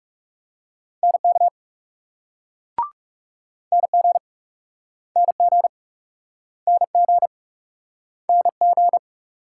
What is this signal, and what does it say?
Morse code, heard as a keyed tone, about 37 wpm, 700 Hz. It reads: NK NG NG NG NG
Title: Speed-Racing Multi-letter Phonemes - 30wpm